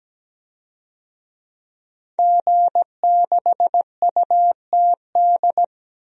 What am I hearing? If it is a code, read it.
Morse code, heard as a keyed tone, about 17 wpm, 700 Hz. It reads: G6UTD
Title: DXCC-100 Call Signs - 5 Characters - 17wpm